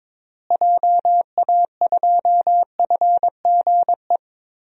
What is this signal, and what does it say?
Morse code, heard as a keyed tone, about 22 wpm, 700 Hz. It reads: JA2FGE
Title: DXCC-100 Call Signs - 6 Characters - Repeated 2x - 22wpm